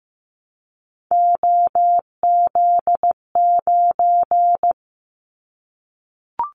Morse code, heard as a keyed tone, about 15 wpm, 700 Hz. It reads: OZ9
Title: Callsign Prefixes for Top 50 DXCC - 15wpm